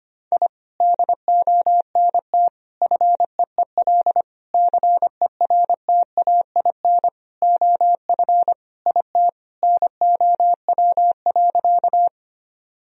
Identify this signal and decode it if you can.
Morse code, heard as a keyed tone, about 25 wpm, 700 Hz. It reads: I DONT FEEL CERTAIN OF IT NOW.
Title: Sentences from Top 1000 Words - Review Part 2 - 25wpm